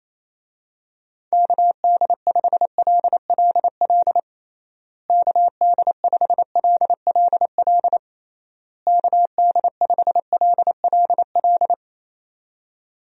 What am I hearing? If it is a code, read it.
Morse code, heard as a keyed tone, about 27 wpm, 700 Hz. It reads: KD5LLL KD5LLL KD5LLL
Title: USA Call Signs - 6 Characters - Repeated 3x - 28wpm